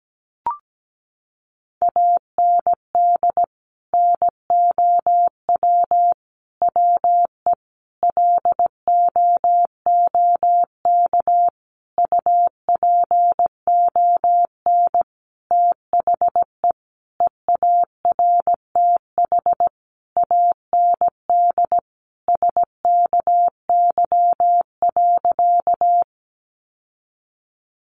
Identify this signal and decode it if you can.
Morse code, heard as a keyed tone, about 17 wpm, 700 Hz. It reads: AND NOW WE LOOK UPON THE EARTH AND SKY.
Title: Sentences from Top 1200 Words 17wpm